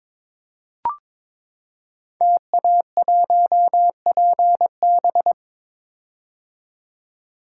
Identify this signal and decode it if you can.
Morse code, heard as a keyed tone, about 21 wpm, 700 Hz. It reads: TA1PB
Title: Code Groups - 5 Letter:Numbers - 22wpm